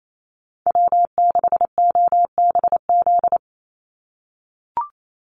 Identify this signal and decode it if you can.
Morse code, heard as a keyed tone, about 28 wpm, 700 Hz. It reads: W6OBZ
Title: USA Call Signs 28wpm